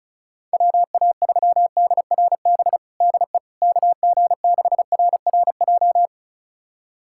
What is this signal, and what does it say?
Morse code, heard as a keyed tone, about 36 wpm, 700 Hz. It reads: WA3DRB DE KG6RRJ
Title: USA Callsign DE Callsign - 6 Characters - 35wpm